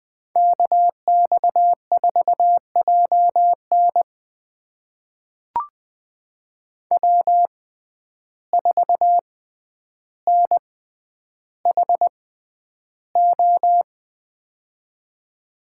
Morse code, heard as a keyed tone, about 20 wpm, spaced as for 10 wpm, 700 Hz. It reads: KX4JN W4NHO
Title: USA Call Signs - 5 Characters - 6x Character Spacing - 20wpm